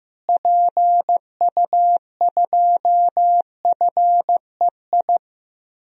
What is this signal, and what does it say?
Morse code, heard as a keyed tone, about 15 wpm, 700 Hz. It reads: PU2FEI
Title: DXCC-100 Call Signs - 6 Characters - Repeated 3x - 15wpm